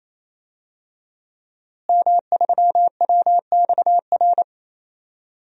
Morse code, heard as a keyed tone, about 27 wpm, 700 Hz. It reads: M3WXR
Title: DXCC-100 Call Signs - 5 Characters - Repeated 3x - 28wpm